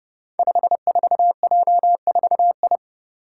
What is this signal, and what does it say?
Morse code, heard as a keyed tone, about 30 wpm, 700 Hz. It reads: 54J4I
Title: Five Challenging Letter-Number Combinations 30wpm